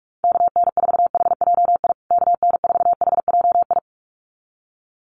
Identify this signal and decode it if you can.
Morse code, heard as a keyed tone, about 44 wpm, 700 Hz. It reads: KN4HJI KN4HJI
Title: USA Call Signs Repeated 2x 45wpm